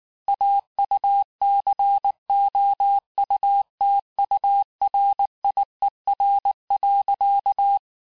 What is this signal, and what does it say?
Morse code, heard as a keyed tone, about 19 wpm, 800 Hz. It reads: AUCOUTURIER.
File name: cours-du-lundi files / cours-1589-texte-clair-vitesse-18mpm-puis-23mpm